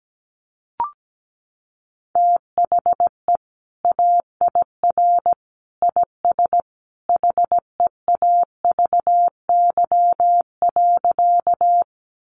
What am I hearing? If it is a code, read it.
Morse code, heard as a keyed tone, about 17 wpm, 700 Hz. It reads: THE AIR IS HEAVY.
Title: Sentences from Top 1100 Words 17wpm